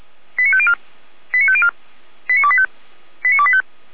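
Five-tone selective call (ZVEI), heard as quick runs of five tones, tones of about 70 ms. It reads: 89504 89504 89285 89285
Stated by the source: ZVEI five-tone selective call received by SDR.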